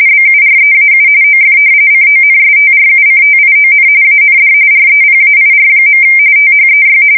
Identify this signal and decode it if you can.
RTTY, about 45.45 baud. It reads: BUREAU OF CONSTRUCTION AND REPAIR. THEY
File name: fsk170 / battleshipnj